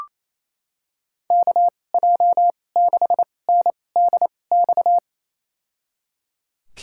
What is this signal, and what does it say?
Morse code, heard as a keyed tone, about 28 wpm, spaced as for 20 wpm, 700 Hz. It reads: KJ6NDX
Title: USA Call Signs - 6 Characters - 2x Character Spacing - 28wpm